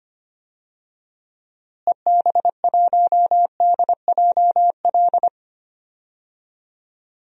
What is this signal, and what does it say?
Morse code, heard as a keyed tone, about 25 wpm, 700 Hz. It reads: EB1DJL
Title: DXCC-100 Call Signs - 6 Characters - Repeated 2x - 25wpm